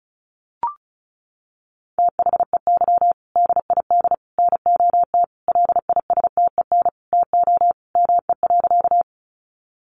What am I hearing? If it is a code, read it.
Morse code, heard as a keyed tone, about 34 wpm, 700 Hz. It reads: THEY DID NOT LISTEN TO ME.
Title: Sentences from Top 600 Words 35wpm